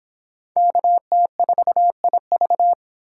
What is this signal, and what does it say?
Morse code, heard as a keyed tone, about 30 wpm, 700 Hz. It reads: KT4IV